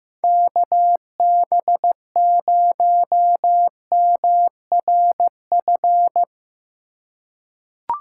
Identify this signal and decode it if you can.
Morse code, heard as a keyed tone, about 15 wpm, 700 Hz. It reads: KB0MRF
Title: USA Call Signs - 6 Characters - 2x Character Spacing - 15wpm